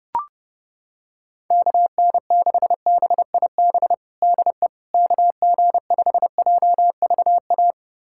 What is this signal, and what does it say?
Morse code, heard as a keyed tone, about 32 wpm, 700 Hz. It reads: KN6BIB DE KG5JVA
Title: USA Callsign DE Callsign - 6 Characters - 30wpm